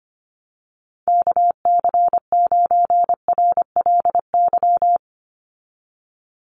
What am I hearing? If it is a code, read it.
Morse code, heard as a keyed tone, about 25 wpm, 700 Hz. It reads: KC9RLY
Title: USA Call Signs 25wpm